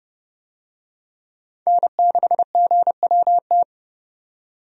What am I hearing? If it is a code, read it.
Morse code, heard as a keyed tone, about 32 wpm, 700 Hz. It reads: N6GWT
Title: USA Call Signs - 5 Characters - 30wpm